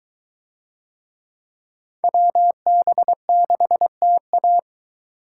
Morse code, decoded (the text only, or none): WB6TA